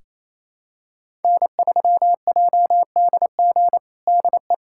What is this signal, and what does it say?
Morse code, heard as a keyed tone, about 28 wpm, 700 Hz. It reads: N3JDG DE
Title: USA Callsign DE Callsign 28wpm